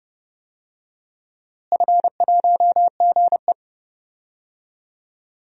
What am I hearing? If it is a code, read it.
Morse code, heard as a keyed tone, about 32 wpm, 700 Hz. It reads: F1GE
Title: DXCC-100 Call Signs - 4 Characters - Repeated 3x - 30wpm